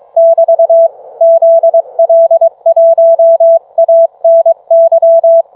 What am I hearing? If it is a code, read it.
Morse code, heard as a keyed tone, about 22 wpm, 660 Hz. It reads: <BT> ZL1ANY